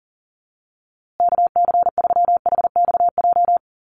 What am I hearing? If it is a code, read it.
Morse code, heard as a keyed tone, about 40 wpm, 700 Hz. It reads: KC3HXJ
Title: USA Call Signs Repeated 3x 40wpm